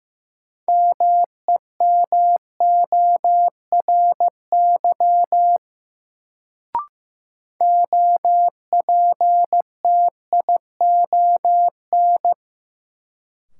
Morse code, heard as a keyed tone, about 15 wpm, 700 Hz. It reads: MEMORY OPTION
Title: Sentences from Top 1300 Words - New Word Drill - 15wpm